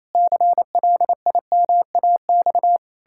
Morse code, heard as a keyed tone, about 28 wpm, 700 Hz. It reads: CLIMAX
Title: Pangram Challenge v2 - 28wpm